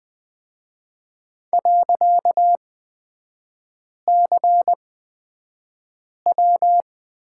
Morse code, heard as a keyed tone, about 20 wpm, spaced as for 5 wpm, 700 Hz. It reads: .CW
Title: 255 Lesson 41e - Period - Three Characters 20wpm